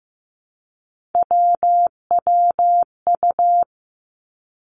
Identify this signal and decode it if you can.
Morse code, heard as a keyed tone, about 15 wpm, 700 Hz. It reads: WWU
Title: Three Challenging Letter-Number Combinations 15wpm